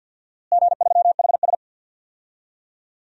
Morse code, heard as a keyed tone, about 52 wpm, 700 Hz. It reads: G3HS
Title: DXCC-100 Call Signs - 4 Characters - Repeated 3x - 50wpm